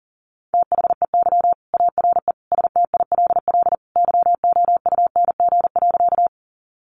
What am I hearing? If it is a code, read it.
Morse code, heard as a keyed tone, about 40 wpm, 700 Hz. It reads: THEY ARE STILL YOUNG.
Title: Sentences from Top 400 Words 40wpm